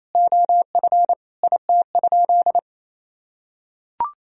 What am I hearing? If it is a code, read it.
Morse code, heard as a keyed tone, about 28 wpm, 700 Hz. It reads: OF IT?
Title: Sentences from Top 1000 Words - Review Part 4 - 28wpm